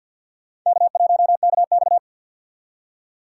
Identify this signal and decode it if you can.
Morse code, heard as a keyed tone, about 50 wpm, 700 Hz. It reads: K1KX
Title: USA Call Signs - 4 Characters - 6x Character Spacing - 50wpm